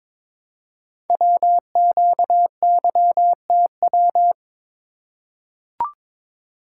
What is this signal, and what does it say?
Morse code, heard as a keyed tone, about 23 wpm, 700 Hz. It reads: WQYTW
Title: Code Groups - 5 Letters - 22wpm